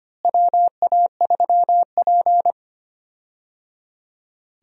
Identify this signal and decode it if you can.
Morse code, heard as a keyed tone, about 25 wpm, 700 Hz. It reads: WA3P
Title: USA Call Signs - 4 Characters - Repeated 3x - 25wpm